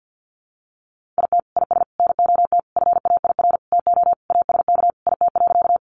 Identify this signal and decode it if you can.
Morse code, heard as a keyed tone, about 50 wpm, 700 Hz. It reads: IT IS NOT FAIR TO ASK IT.
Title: Sentences from Top 1300 Words 50wpm